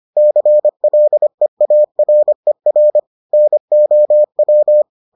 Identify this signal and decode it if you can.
Morse code, heard as a keyed tone, about 25 wpm, 600 Hz. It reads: CLEARER NOW